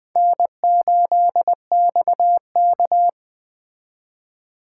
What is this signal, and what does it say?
Morse code, heard as a keyed tone, about 20 wpm, 700 Hz. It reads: N8XK
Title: Speed Racing - USA 1x2 Call Signs 17wpm